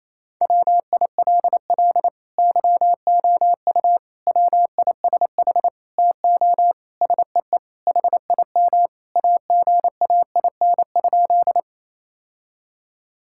Morse code, heard as a keyed tone, about 28 wpm, 700 Hz. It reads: WILL YOU WISH TO SEE HIM AGAIN?
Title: Sentences from Top 1600 Words 28wpm